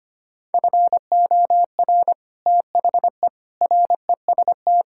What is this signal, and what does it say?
Morse code, heard as a keyed tone, about 25 wpm, 700 Hz. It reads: FOR THE REST
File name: practice / 25wpm / Sentences with 3-Word Phrases - 25wpm